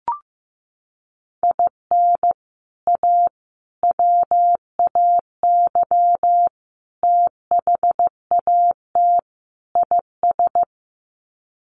Morse code, heard as a keyed tone, about 15 wpm, 700 Hz. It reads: IN A WAY THAT IS
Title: Sets of 5 Words Encore 15wpm